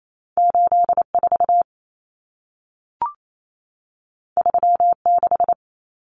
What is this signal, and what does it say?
Morse code, heard as a keyed tone, about 28 wpm, 700 Hz. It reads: 84 36
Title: Common Cut Numbers - 2-Digit 28wpm